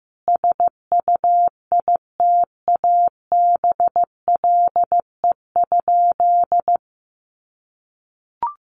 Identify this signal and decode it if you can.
Morse code, heard as a keyed tone, about 15 wpm, 700 Hz. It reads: SUITABLE?
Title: Sentences from Top 1400 Words 15wpm